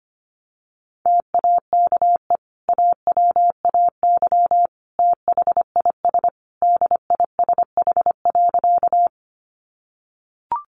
Morse code, heard as a keyed tone, about 25 wpm, 700 Hz. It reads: TAKE AWAY THIS DISH.
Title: Sentences from Top 1300 Words 25wpm